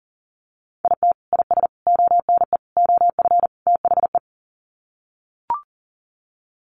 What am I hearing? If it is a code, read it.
Morse code, heard as a keyed tone, about 37 wpm, 700 Hz. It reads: IT IS ONE OF THE
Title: Sets of 5 Words 40wpm